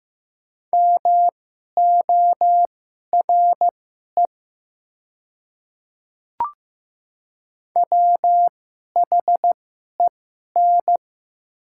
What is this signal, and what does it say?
Morse code, heard as a keyed tone, about 15 wpm, spaced as for 11 wpm, 700 Hz. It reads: MORE WHEN
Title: 2x Character Spacing - Top 200 Words - 15wpm